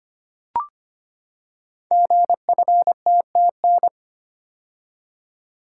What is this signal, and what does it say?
Morse code, heard as a keyed tone, about 25 wpm, 700 Hz. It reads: GFTTN
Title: Code Groups - 5 Letters - 25wpm